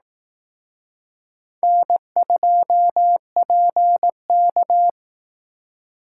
Morse code, heard as a keyed tone, about 18 wpm, 700 Hz. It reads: N2PK
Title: Speed Racing - USA 1x2 Call Signs 15wpm